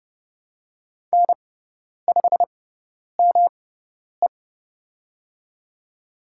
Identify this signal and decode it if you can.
Morse code, heard as a keyed tone, about 30 wpm, spaced as for 10 wpm, 690 Hz. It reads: N5ME
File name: Lesson 18g - Letter m - Call Signs 30wpm